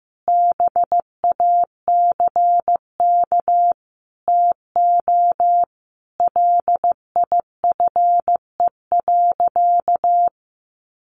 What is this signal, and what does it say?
Morse code, heard as a keyed tone, about 15 wpm, 700 Hz. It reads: BACK TO LIFE.
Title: Sentences from Top 600 Words 15wpm